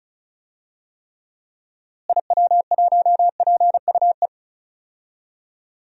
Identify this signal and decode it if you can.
Morse code, heard as a keyed tone, about 36 wpm, 700 Hz. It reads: IW1PUE
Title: DXCC-100 Call Signs - 6 Characters - Repeated 3x - 35wpm